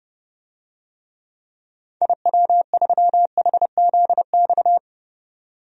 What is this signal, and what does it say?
Morse code, heard as a keyed tone, about 32 wpm, 700 Hz. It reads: IW3HZX
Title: DXCC-100 Call Signs - 6 Characters - 30wpm